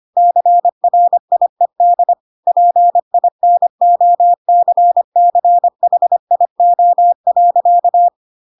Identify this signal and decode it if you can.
Morse code, heard as a keyed tone, about 25 wpm, 700 Hz. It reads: CRIED PINOCCHIO.
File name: Pinocchio / Pinocchio ch4 25 25 ik1hge